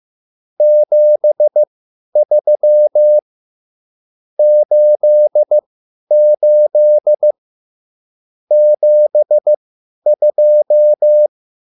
Morse code, heard as a keyed tone, about 15 wpm, spaced as for 10 wpm, 600 Hz. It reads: 73 88 72
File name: QSL / CW lesson